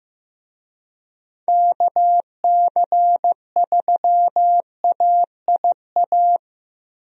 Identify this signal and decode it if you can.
Morse code, heard as a keyed tone, about 16 wpm, 700 Hz. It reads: KC3AIA
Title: USA Call Signs - 6 Characters - Repeated 3x - 15wpm